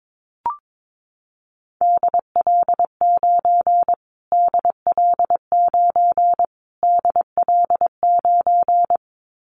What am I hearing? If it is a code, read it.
Morse code, heard as a keyed tone, about 22 wpm, 700 Hz. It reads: DL9 DL9 DL9
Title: Callsign Prefixes for Top 25 DXCC - Repeated 3x - 22wpm